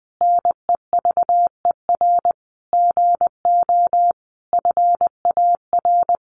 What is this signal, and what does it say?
Morse code, heard as a keyed tone, about 20 wpm, 700 Hz. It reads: NEVER GO FAR
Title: Sentences from Top 1500 Words 20wpm